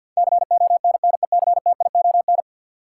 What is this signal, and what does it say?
Morse code, heard as a keyed tone, about 53 wpm, 700 Hz. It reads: CONNECTION
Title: Sentences from Top 1300 Words - New Word Drill Alt - 50wpm